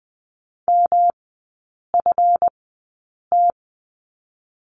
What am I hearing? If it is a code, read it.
Morse code, heard as a keyed tone, about 21 wpm, spaced as for 8 wpm, 700 Hz. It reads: MFT
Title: 138 Lesson 23e - Letter f - Three Characters 20wpm